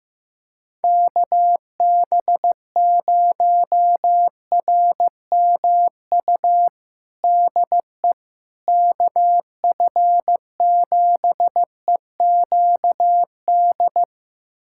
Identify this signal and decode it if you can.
Morse code, heard as a keyed tone, about 15 wpm, 700 Hz. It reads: KB0RMU DE KF7EQD
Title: USA Callsign DE Callsign - 6 Characters - 15wpm